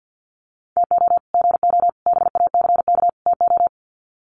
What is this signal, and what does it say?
Morse code, heard as a keyed tone, about 50 wpm, 700 Hz. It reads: TO GO BACK TO
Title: Sets of 4 Words 50wpm